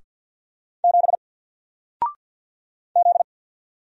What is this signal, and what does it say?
Morse code, heard as a keyed tone, about 50 wpm, 700 Hz. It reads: 7 Z
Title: Single Letter-Number 50wpm